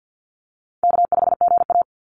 Morse code, heard as a keyed tone, about 50 wpm, 700 Hz. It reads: K5GA
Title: Easy Call Signs Repeated 3x - 50wpm